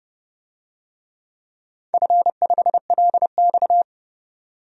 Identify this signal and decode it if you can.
Morse code, heard as a keyed tone, about 30 wpm, 700 Hz. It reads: F5LX
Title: DXCC-100 Call Signs - 4 Characters - 30wpm